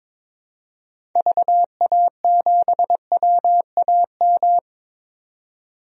Morse code, heard as a keyed tone, about 22 wpm, 700 Hz. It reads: VA7WAM
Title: DXCC-100 Call Signs - 6 Characters - 22wpm